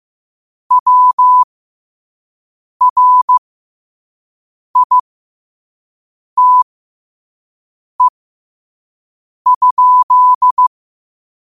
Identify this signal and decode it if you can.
Morse code, heard as a keyed tone, about 15 wpm, spaced as for 5 wpm, 1000 Hz. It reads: WRITE?
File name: qotdfiles / 5wpm